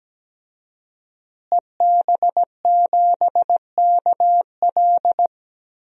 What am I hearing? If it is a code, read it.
Morse code, heard as a keyed tone, about 17 wpm, 700 Hz. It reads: EB7KL
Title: DXCC-100 Call Signs - 5 Characters - Repeated 2x - 17wpm